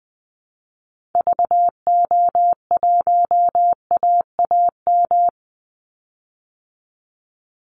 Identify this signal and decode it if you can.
Morse code, heard as a keyed tone, about 20 wpm, 700 Hz. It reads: VO1AAM
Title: Call Signs Repeated 2x - 20wpm